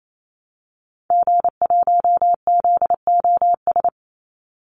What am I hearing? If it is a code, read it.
Morse code, heard as a keyed tone, about 28 wpm, 700 Hz. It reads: G1ZOS
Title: Call Signs - 28wpm